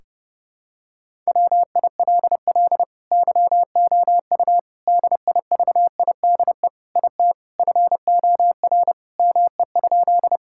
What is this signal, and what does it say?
Morse code, heard as a keyed tone, about 32 wpm, 700 Hz. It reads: WILL YOU DIVIDE IT FOR ME?
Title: Sentences from Top 1600 Words 30wpm